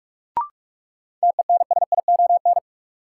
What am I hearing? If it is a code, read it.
Morse code, heard as a keyed tone, about 44 wpm, 700 Hz. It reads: TENSION
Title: Sentences from Top 1600 Words - New Word Drill - 45wpm